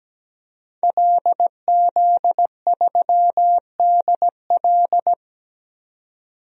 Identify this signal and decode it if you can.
Morse code, heard as a keyed tone, about 17 wpm, 700 Hz. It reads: LZ3DL
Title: DXCC-100 Call Signs - 5 Characters - 17wpm